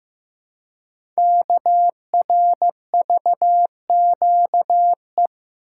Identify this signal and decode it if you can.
Morse code, heard as a keyed tone, about 15 wpm, 700 Hz. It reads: KRVQE
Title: Code Groups - 5 Letters - 15wpm